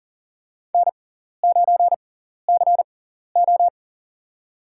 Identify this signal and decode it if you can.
Morse code, heard as a keyed tone, about 40 wpm, spaced as for 14 wpm, 700 Hz. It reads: N9CO
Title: USA Call Signs - 4 Characters - 6x Character Spacing - 40wpm